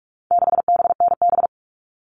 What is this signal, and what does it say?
Morse code, heard as a keyed tone, about 44 wpm, 700 Hz. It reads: 6BNB
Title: All Cut Numbers - 4-Digit 45wpm